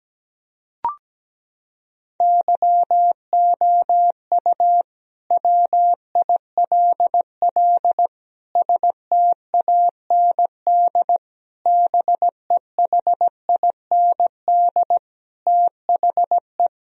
Morse code, heard as a keyed tone, about 17 wpm, 700 Hz. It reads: YOU WILL STAND BEHIND THE
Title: Sentences from Top 1000 Words - Review Part 4 - 17wpm